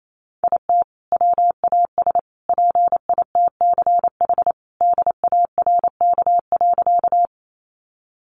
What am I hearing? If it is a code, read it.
Morse code, heard as a keyed tone, about 28 wpm, 700 Hz. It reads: IT WAS PITCH DARK.
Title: Sentences from Top 1300 Words 28wpm